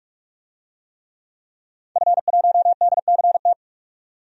Morse code, heard as a keyed tone, about 44 wpm, 700 Hz. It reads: F1DCT
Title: DXCC-100 Call Signs - 5 Characters - 45wpm